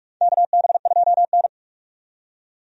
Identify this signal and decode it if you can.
Morse code, heard as a keyed tone, about 46 wpm, 700 Hz. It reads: KB2N
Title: USA Call Signs - 4 Characters - 45wpm